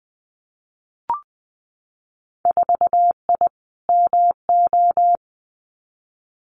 Morse code, heard as a keyed tone, about 20 wpm, 700 Hz. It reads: 4I MO
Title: Winter Field Day Exchanges 20wpm